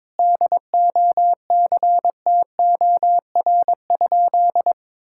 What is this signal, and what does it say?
Morse code, heard as a keyed tone, about 22 wpm, 700 Hz. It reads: DOCTOR?
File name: practice / 22wpm / Sentences with 4-Word Phrases - 22wpm